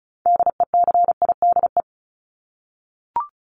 Morse code, heard as a keyed tone, about 34 wpm, 700 Hz. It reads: DECIDE
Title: ED Suffix - Top 100 Words with Base Words 35wpm